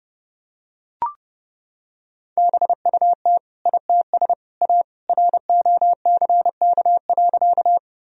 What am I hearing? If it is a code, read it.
Morse code, heard as a keyed tone, about 32 wpm, 700 Hz. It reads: BUT ITS A ROCK.
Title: Sentences from Top 1000 Words - Review Part 2 - 30wpm